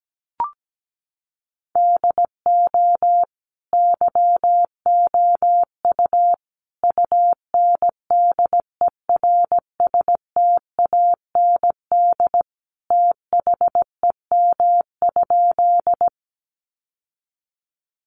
Morse code, heard as a keyed tone, about 17 wpm, 700 Hz. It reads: DO YOU UNDERSTAND THEM?
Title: Sentences from Top 600 Words 17wpm